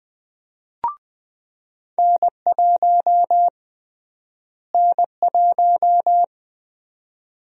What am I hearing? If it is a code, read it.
Morse code, heard as a keyed tone, about 20 wpm, 700 Hz. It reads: N1 N1
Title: USA Call Sign Prefixes - Repeated 2x - 20wpm